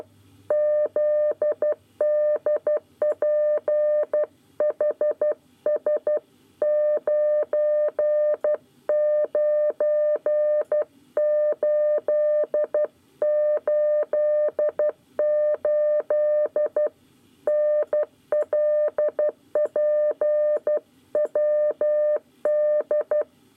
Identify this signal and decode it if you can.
Morse code, heard as a keyed tone, about 12 wpm, 600 Hz. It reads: ZDPHS 99888 NLPWD
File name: 9 lezione cw 1